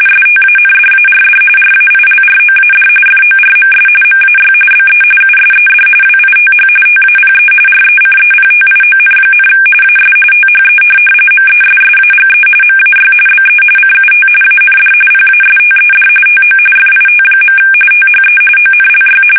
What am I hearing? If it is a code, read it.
RTTY, about 45.45 baud. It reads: STANDARD RAILROAD FORMS MAY BE USED WITH TAPE RECEPTION, AS THE MESSAGE MAY BE CUT AND PASTED TO FIT. THE ONLY
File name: fsk850